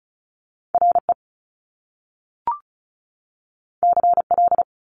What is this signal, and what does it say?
Morse code, heard as a keyed tone, about 35 wpm, 700 Hz. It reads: RE CL
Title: Top Two Letter Combinations in Words 35wpm